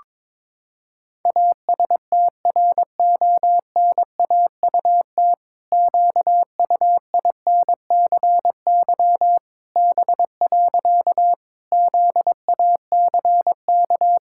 Morse code, heard as a keyed tone, about 22 wpm, 700 Hz. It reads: ASTRONAUT QUINCY B. ZACK
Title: Pangram Challenge v2 - 22wpm